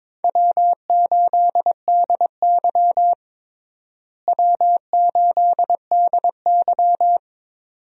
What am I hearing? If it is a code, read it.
Morse code, heard as a keyed tone, about 22 wpm, 700 Hz. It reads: W8DY W8DY
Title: USA Call Signs - 4 Characters - Repeated 2x - 22wpm